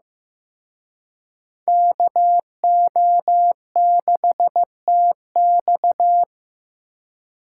Morse code, heard as a keyed tone, about 18 wpm, 700 Hz. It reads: KO6TX
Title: Speed Racing - USA 2x2 Call Signs 15wpm